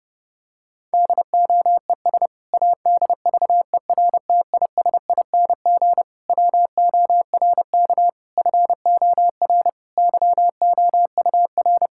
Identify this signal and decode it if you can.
Morse code, heard as a keyed tone, about 32 wpm, 700 Hz. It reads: DOES ADVERTISING WORK FOR YOUR
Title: Sentences from Top 1000 Words - Review Part 6 - 30wpm